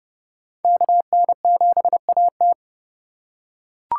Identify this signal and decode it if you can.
Morse code, heard as a keyed tone, about 30 wpm, 700 Hz. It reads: KN7AT
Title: USA Call Signs - 5 Characters - 2x Character Spacing - 30wpm